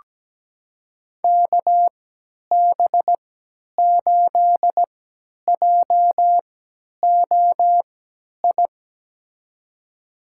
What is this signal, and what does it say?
Morse code, heard as a keyed tone, about 17 wpm, spaced as for 10 wpm, 700 Hz. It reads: KB8JOI
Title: USA Call Signs - 6 Characters - 3x Character Spacing - 17wpm